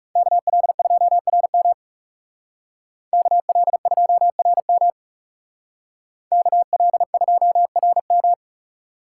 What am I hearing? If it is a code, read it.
Morse code, heard as a keyed tone, about 38 wpm, 700 Hz. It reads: KL2RM KL2RM KL2RM